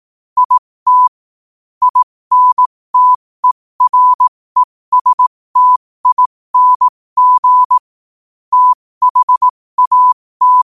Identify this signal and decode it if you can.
Morse code, heard as a keyed tone, about 18 wpm, spaced as for 15 wpm, 1000 Hz. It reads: IT INTERESTING THAT
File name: qotdfiles / 15wpm